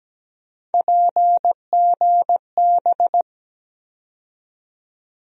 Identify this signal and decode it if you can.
Morse code, heard as a keyed tone, about 17 wpm, 700 Hz. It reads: PGB
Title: USA Call Sign Postfixes - 3 Letters 17wpm